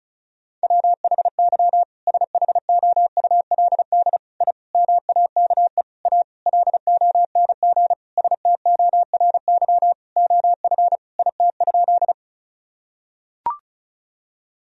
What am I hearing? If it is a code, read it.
Morse code, heard as a keyed tone, about 36 wpm, 700 Hz. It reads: WHY SHOULD I MAKE A LONG STORY OF IT?
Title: Sentences from Top 1000 Words - Review Part 1 - 35wpm